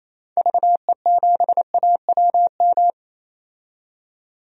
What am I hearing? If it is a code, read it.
Morse code, heard as a keyed tone, about 28 wpm, 700 Hz. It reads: VE7AWM
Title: DXCC-100 Call Signs - 6 Characters - 28wpm